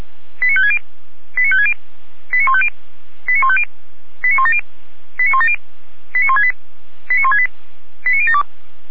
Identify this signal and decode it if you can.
Five-tone selective call (ZVEI), heard as quick runs of five tones, tones of about 70 ms. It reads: 89560 89560 89150 89150 89170 89170 89168 89168 89072